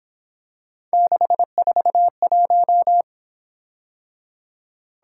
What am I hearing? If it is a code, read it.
Morse code, heard as a keyed tone, about 26 wpm, 700 Hz. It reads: 641
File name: Speed-Racing - Three Letter-Number Combinations - 22wpm